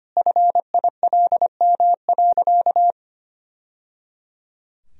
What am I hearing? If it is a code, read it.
Morse code, heard as a keyed tone, about 25 wpm, 700 Hz. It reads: FILM.
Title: Sentences from Top 1000 Words - Review Part 3 - 25wpm